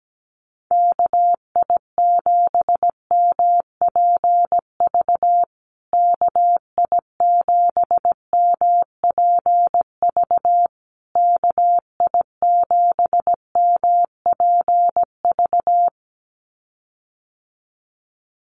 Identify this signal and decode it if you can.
Morse code, heard as a keyed tone, about 17 wpm, 700 Hz. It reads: KI7MPV KI7MPV KI7MPV
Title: USA Call Signs Repeated 3x 17wpm